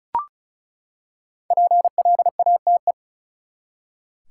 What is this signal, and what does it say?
Morse code, heard as a keyed tone, about 36 wpm, 700 Hz. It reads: PLATE
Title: Five Letter Words - 35wpm